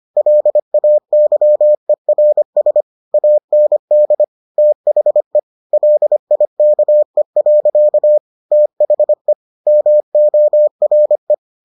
Morse code, heard as a keyed tone, about 25 wpm, 600 Hz. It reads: LAYERS AND THE LIKE. THE MORE